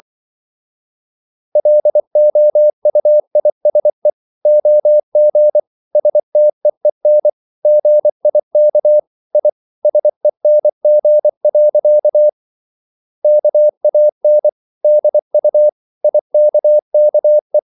Morse code, heard as a keyed tone, about 24 wpm, 600 Hz. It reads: LOUISE OG STEEN GIK I SENG. KAN DU IKKE
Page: Ferie 24 wpm